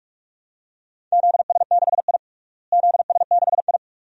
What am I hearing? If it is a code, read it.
Morse code, heard as a keyed tone, about 46 wpm, 700 Hz. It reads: ZS6I ZS6I
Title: DXCC-100 Call Signs - 4 Characters - Repeated 2x - 45wpm